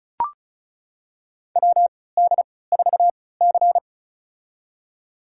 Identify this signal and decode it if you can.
Morse code, heard as a keyed tone, about 34 wpm, spaced as for 20 wpm, 700 Hz. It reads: WD4C
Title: USA Call Signs - 4 Characters - 3x Character Spacing - 35wpm